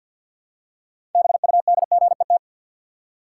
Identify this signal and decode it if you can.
Morse code, heard as a keyed tone, about 50 wpm, 700 Hz. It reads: BUDGET
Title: Six Letter Words - 50wpm